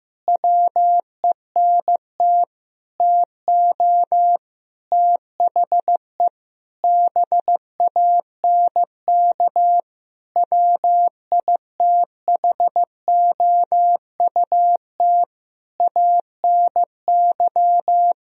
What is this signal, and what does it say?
Morse code, heard as a keyed tone, about 15 wpm, 700 Hz. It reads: WENT TO THE BANK WITHOUT ANY
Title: Sentences from Top 1000 Words - Review Part 4 - 15wpm